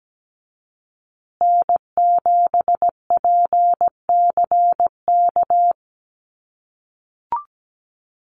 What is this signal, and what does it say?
Morse code, heard as a keyed tone, about 17 wpm, 700 Hz. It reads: N7PCK
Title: USA Call Signs Repeated 2x 17wpm